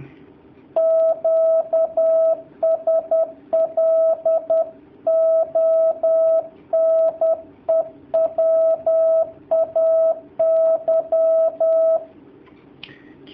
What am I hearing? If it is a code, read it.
Morse code, heard as a keyed tone, about 10 wpm, 650 Hz. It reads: QSL ONE WAY
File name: jh8cbh / 2019cw / sp10- w